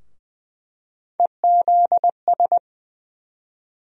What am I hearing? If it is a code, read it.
Morse code, heard as a keyed tone, about 20 wpm, 700 Hz. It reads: EZS
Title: USA Call Sign Postfixes - 3 Letters 20wpm